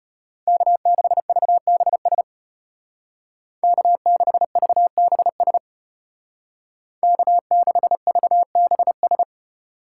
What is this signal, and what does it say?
Morse code, heard as a keyed tone, about 30 wpm, 700 Hz. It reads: K6VBS K6VBS K6VBS
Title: Speed Racing - USA 1x3 Call Signs 25wpm